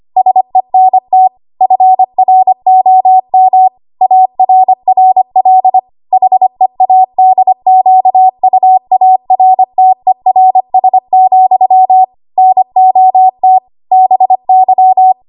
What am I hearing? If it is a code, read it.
Morse code, heard as a keyed tone, about 25 wpm, 750 Hz. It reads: SENT FROM ARRL HEADQUARTERS, NOT BY
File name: Archive / 25 WPM / 25WPM